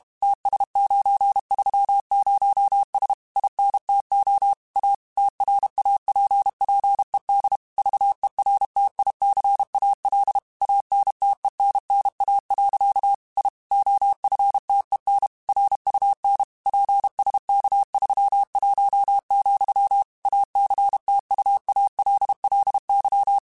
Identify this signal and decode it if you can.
Morse code, heard as a keyed tone, about 32 wpm, 800 Hz. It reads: TS930S INTO A TRAPPED VERTICAL ANTENNA. I OFTEN RUN PSK31, ACTUALLY
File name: QSOs / QSOs-mp3s / kc7sac@30